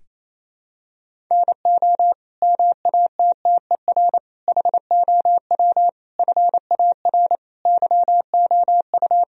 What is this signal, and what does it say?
Morse code, heard as a keyed tone, about 28 wpm, 700 Hz. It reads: NO MATTER HOW FAR YOU
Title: Sentences from Top 1000 Words - Review Part 1 - 28wpm